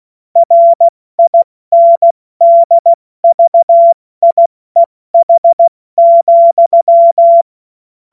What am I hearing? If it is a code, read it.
Morse code, heard as a keyed tone, about 16 wpm, 680 Hz. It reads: RINDVIEH,
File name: Morsekurs / Witze / Witz 01. Februar 2022 70 F80